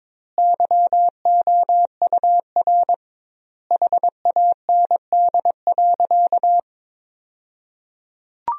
Morse code, heard as a keyed tone, about 22 wpm, 700 Hz. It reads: YOUR HAND.
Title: 2x Word-Spacing - Sentences from Top 200 Words - 22wpm